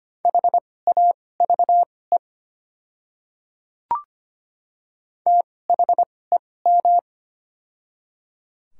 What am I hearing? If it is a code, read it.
Morse code, heard as a keyed tone, about 25 wpm, spaced as for 18 wpm, 700 Hz. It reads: HAVE THEM
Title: 2x Character Spacing - Top 100 Words - 25wpm